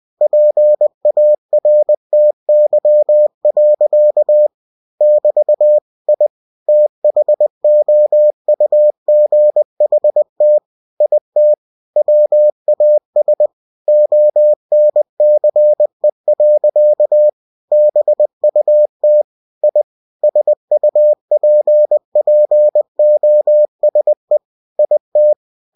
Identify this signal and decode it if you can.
Morse code, heard as a keyed tone, about 20 wpm, 600 Hz. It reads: PARTY. <BT> I THOUGHT IT WAS ONCE. BUT I SUPPOSE IT
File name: cw-books / winnie-the-pooh / winnie-20wpm-page-0108